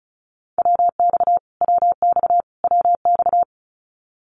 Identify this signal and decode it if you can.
Morse code, heard as a keyed tone, about 34 wpm, 700 Hz. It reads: WX WX WX
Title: Abbreviations Repeated 3x - 35wpm